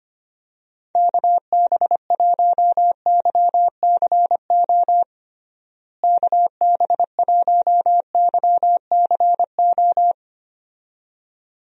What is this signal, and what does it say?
Morse code, heard as a keyed tone, about 25 wpm, 700 Hz. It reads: KB1YCO KB1YCO
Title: USA Call Signs - 6 Characters - Repeated 2x - 25wpm